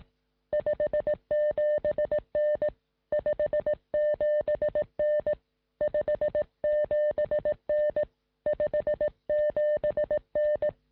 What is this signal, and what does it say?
Morse code, heard as a keyed tone, about 18 wpm, 600 Hz. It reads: 57N 57N 57N 57N
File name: sm7lcb / 1958z ok1ri 2m